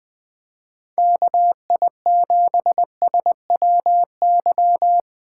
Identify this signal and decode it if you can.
Morse code, heard as a keyed tone, about 20 wpm, 700 Hz. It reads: KI7SWY
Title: Speed Racing - USA 2x3 Call Signs 17wpm